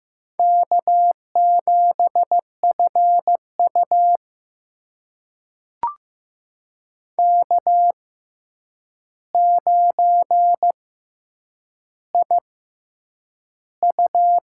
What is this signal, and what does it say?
Morse code, heard as a keyed tone, about 15 wpm, spaced as for 8 wpm, 700 Hz. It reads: K7FU K9IU
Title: USA Call Signs - 4 Characters - 6x Character Spacing - 15wpm